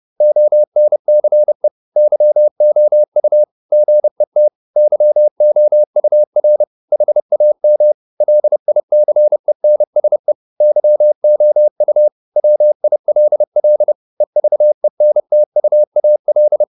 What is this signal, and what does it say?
Morse code, heard as a keyed tone, about 30 wpm, 600 Hz. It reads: ONCE YOU GET YOUR HAM LICENSE YOU WILL EVENTUAL